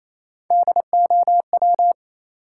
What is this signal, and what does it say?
Morse code, heard as a keyed tone, about 27 wpm, 700 Hz. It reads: DOW
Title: Sets of 5 Words - New Word Drill - 28wpm